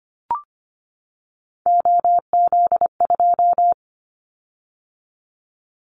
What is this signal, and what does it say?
Morse code, heard as a keyed tone, about 25 wpm, 700 Hz. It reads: OZ2
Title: Callsign Prefixes for Top 50 DXCC - 25wpm